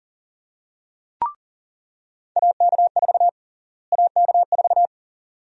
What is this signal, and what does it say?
Morse code, heard as a keyed tone, about 40 wpm, 700 Hz. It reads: AK4 AK4
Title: USA Call Sign Prefixes - Repeated 2x - 40wpm